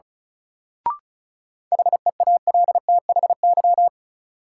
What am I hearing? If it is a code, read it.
Morse code, heard as a keyed tone, about 36 wpm, 700 Hz. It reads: HEALTHY
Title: Sentences from Top 1000 Words - Review Part 5 - New Word Drill - 35wpm